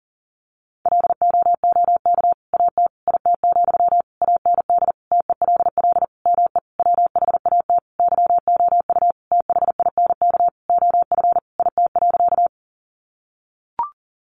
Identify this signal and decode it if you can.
Morse code, heard as a keyed tone, about 40 wpm, 700 Hz. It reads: LOOK AT IT, AND TELL ME WHAT YOU THINK OF IT.
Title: Sentences from Top 200 Words 40wpm